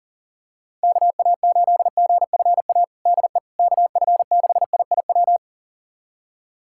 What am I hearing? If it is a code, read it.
Morse code, heard as a keyed tone, about 37 wpm, 700 Hz. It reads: KA8GFA DE KF6IIW
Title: USA Callsign DE Callsign 40wpm